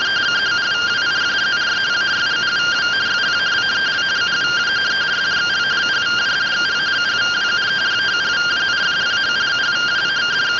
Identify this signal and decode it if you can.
Two-tone teleprinter signal, about 170 Hz shift.
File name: findings / axial iodine 10mm 184.gen-32 rtty rx